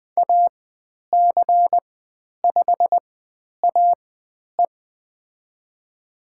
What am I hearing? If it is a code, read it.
Morse code, heard as a keyed tone, about 20 wpm, spaced as for 10 wpm, 700 Hz. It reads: AC5AE
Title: 095 Lesson 16g - Letter c - Call Signs 20wpm